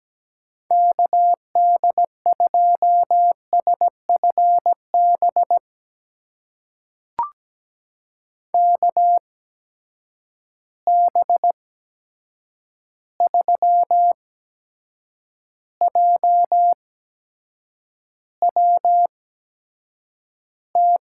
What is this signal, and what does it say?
Morse code, heard as a keyed tone, about 17 wpm, spaced as for 7 wpm, 700 Hz. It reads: KD2SFB KB3JWT
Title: USA Call Signs - 6 Characters - 8x Character Spacing - 17wpm